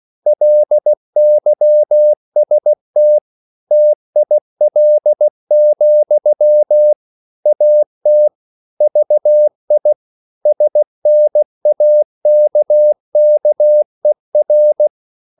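Morse code, heard as a keyed tone, about 16 wpm, 600 Hz. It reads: LYST TIL, AT VI SNAKKER
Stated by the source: Slow (QRS) Morse code at 16 wpm, from CW med Gnister.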